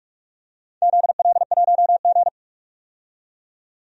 Morse code, heard as a keyed tone, about 41 wpm, 700 Hz. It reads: ZL1G
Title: DXCC-100 Call Signs - 4 Characters - 45wpm